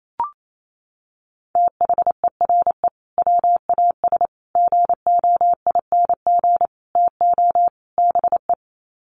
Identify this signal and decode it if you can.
Morse code, heard as a keyed tone, about 28 wpm, 700 Hz. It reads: THERE WAS GOING TO BE
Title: Sets of 5 Words Encore2 28wpm